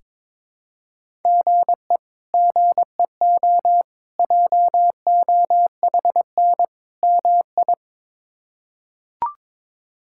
Morse code, heard as a keyed tone, about 22 wpm, 700 Hz. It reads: GE GEO JOHN MI
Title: K1USN SST QSO Practice - 22wpm